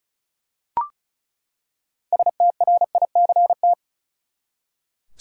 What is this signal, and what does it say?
Morse code, heard as a keyed tone, about 36 wpm, 700 Hz. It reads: STRICT
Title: Six Letter Words - 35wpm